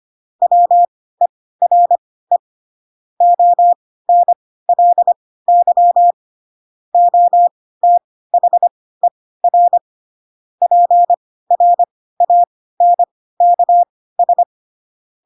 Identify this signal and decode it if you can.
Morse code, heard as a keyed tone, about 25 wpm, spaced as for 16 wpm, 700 Hz. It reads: WERE ONLY OTHER PRANKS
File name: Pinocchio / Pinocchio ch7 10 25 ik1hge